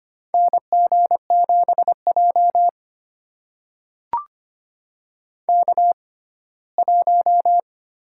Morse code, heard as a keyed tone, about 25 wpm, spaced as for 17 wpm, 700 Hz. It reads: NG7J K1
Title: USA Call Signs - 4 Characters - 6x Character Spacing - 25wpm